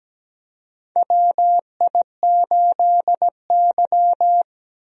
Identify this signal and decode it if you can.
Morse code, heard as a keyed tone, about 17 wpm, 700 Hz. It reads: WI8Y
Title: USA Call Signs - 4 Characters - Repeated 3x - 17wpm